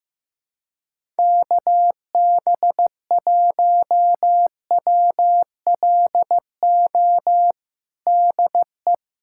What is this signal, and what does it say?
Morse code, heard as a keyed tone, about 15 wpm, 700 Hz. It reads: KB1WLO DE
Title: USA Callsign DE Callsign 15wpm